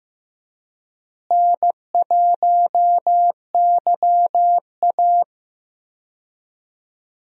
Morse code, heard as a keyed tone, about 15 wpm, 700 Hz. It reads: N1YA
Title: USA Call Signs - 4 Characters - Repeated 3x - 15wpm